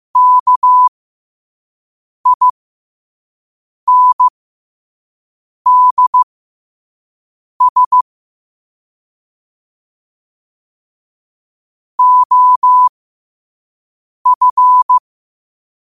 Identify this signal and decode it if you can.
Morse code, heard as a keyed tone, about 15 wpm, spaced as for 5 wpm, 1000 Hz. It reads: KINDS OF
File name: qotdfiles / 5wpm